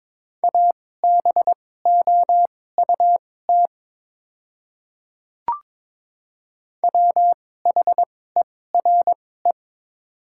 Morse code, heard as a keyed tone, about 22 wpm, spaced as for 16 wpm, 690 Hz. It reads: ABOUT WHERE
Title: 2x Character Spacing - Top 300 Words - 22wpm